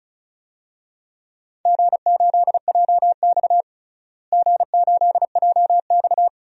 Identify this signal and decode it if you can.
Morse code, heard as a keyed tone, about 36 wpm, 700 Hz. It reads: G8JX G8JX
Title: DXCC-100 Call Signs - 4 Characters - Repeated 2x - 35wpm